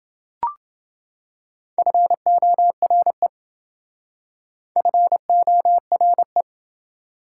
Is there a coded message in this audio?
Yes: Morse code from 1.8 to 6.4 s, heard as a keyed tone, about 27 wpm, 700 Hz. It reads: FORE FORE